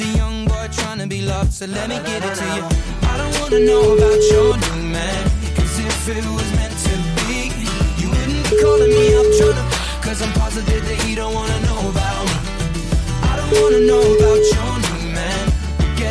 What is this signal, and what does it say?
Ringback tone (Europe, 425 Hz), with noise.